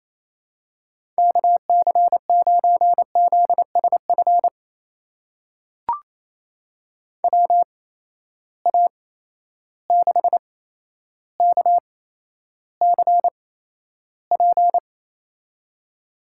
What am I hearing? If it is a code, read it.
Morse code, heard as a keyed tone, about 28 wpm, spaced as for 12 wpm, 700 Hz. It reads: KC9ZSF WA6KCP
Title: USA Call Signs - 6 Characters - 8x Character Spacing - 28wpm